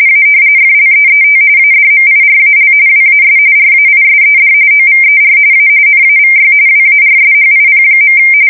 RTTY, about 45.45 baud. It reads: A FLEET ADMIRAL WAS RAISED OVER A NAVY WARSHIP.
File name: fsk170 / battleshipnj